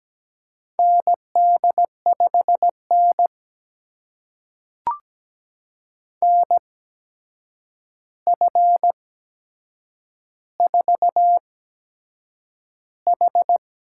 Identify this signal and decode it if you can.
Morse code, heard as a keyed tone, about 17 wpm, spaced as for 7 wpm, 700 Hz. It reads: ND5N NF4H
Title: USA Call Signs - 4 Characters - 8x Character Spacing - 17wpm